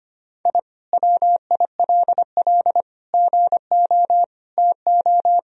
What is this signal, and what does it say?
Morse code, heard as a keyed tone, about 25 wpm, 700 Hz. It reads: I WILL GO TO
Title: Sentences from Top 1000 Words - Review Part 2 - 25wpm